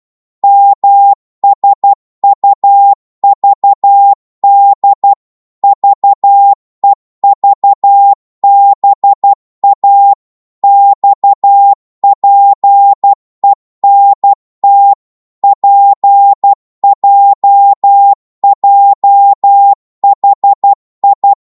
Morse code, heard as a keyed tone, about 12 wpm, 800 Hz. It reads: MSUVD VEVBA XPENT PJJHI
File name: Morse / ltr-12wpm